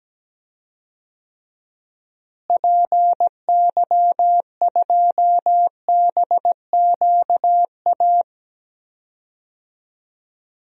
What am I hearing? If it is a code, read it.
Morse code, heard as a keyed tone, about 17 wpm, 700 Hz. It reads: PY2BQA
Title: DXCC-100 Call Signs - 6 Characters - Repeated 3x - 17wpm